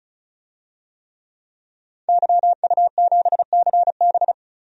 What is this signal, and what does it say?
Morse code, heard as a keyed tone, about 34 wpm, 700 Hz. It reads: YU7CB
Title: DXCC-100 Call Signs - 5 Characters - 35wpm